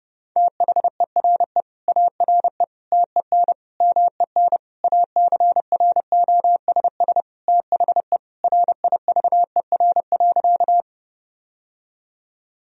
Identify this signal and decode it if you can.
Morse code, heard as a keyed tone, about 30 wpm, 700 Hz. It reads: THERE ARE TEN MEN ACROSS THE RIVER.
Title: Sentences from Top 1000 Words - Review Part 2 - 30wpm